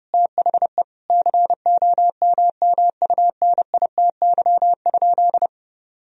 Morse code, heard as a keyed tone, about 30 wpm, 700 Hz. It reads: THE COMMUNITY?
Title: Sentences from Top 1000 Words - Review Part 4 - 30wpm